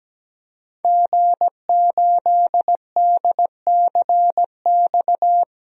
Morse code, heard as a keyed tone, about 17 wpm, 700 Hz. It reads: G8DCX
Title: DXCC-100 Call Signs - 5 Characters - 17wpm